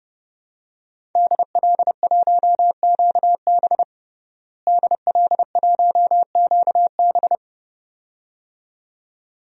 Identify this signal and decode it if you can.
Morse code, heard as a keyed tone, about 30 wpm, 700 Hz. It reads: DL1QB DL1QB
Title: DXCC-100 Call Signs - 5 Characters - Repeated 2x - 30wpm